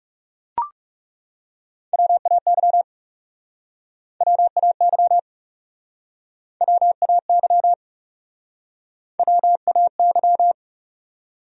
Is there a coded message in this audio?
Yes: Morse code from 1.9 to 10.5 s, heard as a keyed tone, about 41 wpm, 700 Hz. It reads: WAY WAY WAY WAY